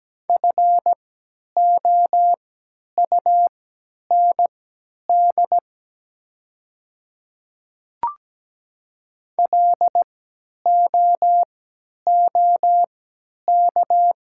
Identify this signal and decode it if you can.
Morse code, heard as a keyed tone, about 17 wpm, spaced as for 10 wpm, 700 Hz. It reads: FOUND LOOK
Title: 3x Character Spacing - Top 300 Words - 17wpm